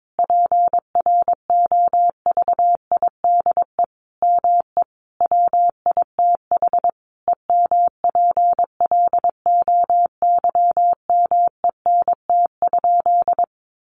Morse code, heard as a keyed tone, about 22 wpm, 700 Hz. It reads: PROVIDE ME WITH EMPLOYMENT?
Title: Sentences from Top 1500 Words 22wpm